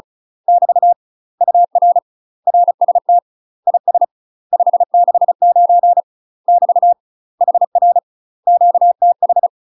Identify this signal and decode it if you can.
Morse code, heard as a keyed tone, about 34 wpm, 700 Hz. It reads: <BT> UR RST IS 569 <BT> HR QTH
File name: UK MASTER QSOs TRAINING / 35WPM / UK MASTER QSOs 0028